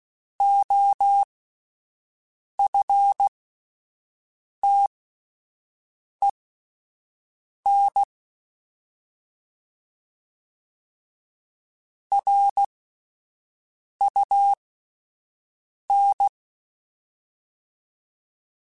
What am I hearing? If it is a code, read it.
Morse code, heard as a keyed tone, about 16 wpm, spaced as for 5 wpm, 800 Hz. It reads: OFTEN RUN